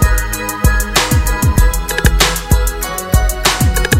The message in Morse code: none